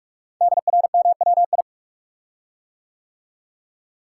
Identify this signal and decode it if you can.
Morse code, heard as a keyed tone, about 44 wpm, 700 Hz. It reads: DRMWI
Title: Code Groups - 5 Letters - 45wpm